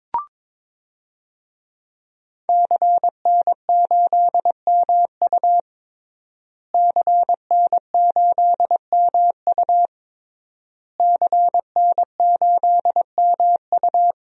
Morse code, heard as a keyed tone, about 22 wpm, 700 Hz. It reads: CN8MU CN8MU CN8MU
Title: DXCC-100 Call Signs - 5 Characters - Repeated 3x - 22wpm